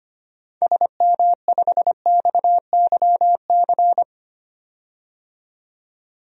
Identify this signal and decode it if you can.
Morse code, heard as a keyed tone, about 25 wpm, 700 Hz. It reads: SM5XYC
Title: DXCC-100 Call Signs - 6 Characters - 25wpm